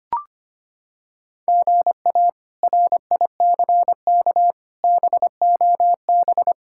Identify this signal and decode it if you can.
Morse code, heard as a keyed tone, about 25 wpm, 700 Hz. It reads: GA RICK BOB
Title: K1USN SST QSO Practice - 25wpm